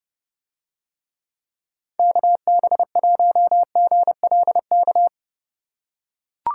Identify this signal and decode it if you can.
Morse code, heard as a keyed tone, about 30 wpm, 700 Hz. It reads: KB1GLK